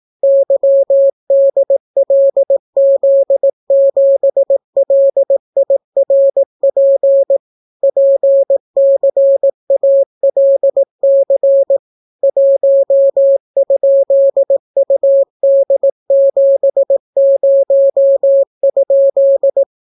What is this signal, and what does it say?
Morse code, heard as a keyed tone, about 18 wpm, 550 Hz. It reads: YDLZ7LIRP PCALC 1?UD70?